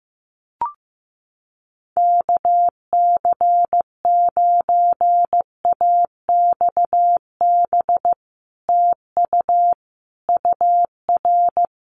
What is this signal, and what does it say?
Morse code, heard as a keyed tone, about 15 wpm, 700 Hz. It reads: KC9AXB TU UR
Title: Common QSO Phrases Encore 15wpm